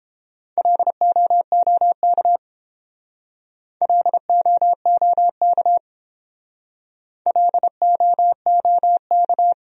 Morse code, heard as a keyed tone, about 32 wpm, 700 Hz. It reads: LOOK LOOK LOOK